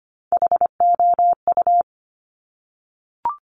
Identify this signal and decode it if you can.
Morse code, heard as a keyed tone, about 25 wpm, 700 Hz. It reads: HOU
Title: Top Three Letter Combinations in Words 25wpm